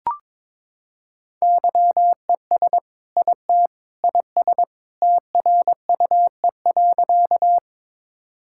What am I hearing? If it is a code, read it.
Morse code, heard as a keyed tone, about 22 wpm, 700 Hz. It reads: YES IT IS TRUE.
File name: practice / 22wpm / Sentences with 2-Word Phrases - 22wpm